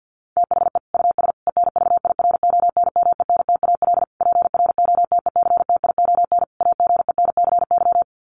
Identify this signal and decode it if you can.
Morse code, heard as a keyed tone, about 50 wpm, 700 Hz. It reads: THE US ENVIRONMENTAL PROTECTION AGENCY
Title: Sets of 5 Words Encore2 50wpm